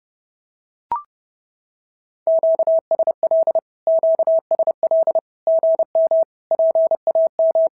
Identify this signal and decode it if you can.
Morse code, heard as a keyed tone, about 30 wpm, 650 Hz. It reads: QSL QSL GM PAM
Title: SOTA QSO Practice 30wpm